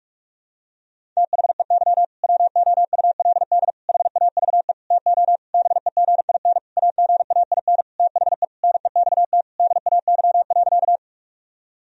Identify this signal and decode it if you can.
Morse code, heard as a keyed tone, about 46 wpm, 700 Hz. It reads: THEY WOULD HAVE TO BEGIN AGAIN THE NEXT DAY.
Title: Sentences from Top 1000 Words - Review Part 1 - 45wpm